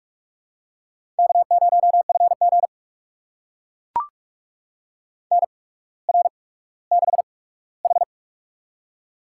Morse code, heard as a keyed tone, about 44 wpm, spaced as for 19 wpm, 700 Hz. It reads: K0FG NR6H
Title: USA Call Signs - 4 Characters - 8x Character Spacing - 45wpm